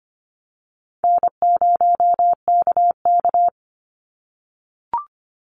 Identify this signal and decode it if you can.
Morse code, heard as a keyed tone, about 25 wpm, 700 Hz. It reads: N0KK
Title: Easy Call Signs - 25wpm